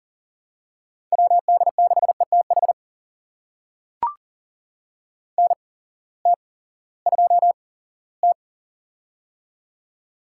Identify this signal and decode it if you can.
Morse code, heard as a keyed tone, about 38 wpm, spaced as for 20 wpm, 700 Hz. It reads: WD6ETH NT2T
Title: USA Call Signs - 8x Character Spacing - 40wpm